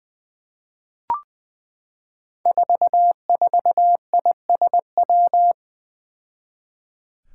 Morse code, heard as a keyed tone, about 20 wpm, 700 Hz. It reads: 44ISW
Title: Five Challenging Letter-Number Combinations 20wpm